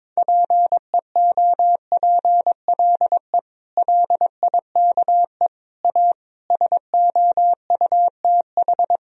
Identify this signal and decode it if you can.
Morse code, heard as a keyed tone, about 22 wpm, 700 Hz. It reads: PEOPLE LIKE A SOUTH
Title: Sentences from Top 1000 Words - Review Part 3 - 22wpm